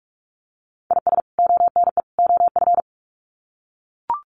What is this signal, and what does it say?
Morse code, heard as a keyed tone, about 44 wpm, 700 Hz. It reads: IS ONE OF
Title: Sets of 3 Words 45wpm